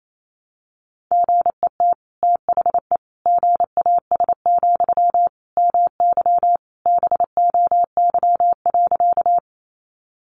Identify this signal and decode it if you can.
Morse code, heard as a keyed tone, about 28 wpm, 700 Hz. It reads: GET THE GAS, MY BOY.
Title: Sentences from Top 1100 Words 28wpm